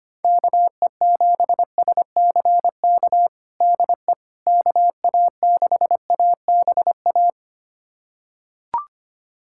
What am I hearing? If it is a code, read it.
Morse code, heard as a keyed tone, about 25 wpm, 700 Hz. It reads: KE7SCK DE KA6ABA
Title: USA Callsign DE Callsign - 6 Characters - 25wpm